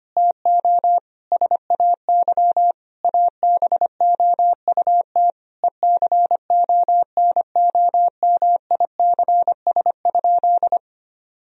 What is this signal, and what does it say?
Morse code, heard as a keyed tone, about 25 wpm, 700 Hz. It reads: TO SAY ABOUT ECONOMICS?
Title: Sentences from Top 1000 Words - Review Part 3 - 25wpm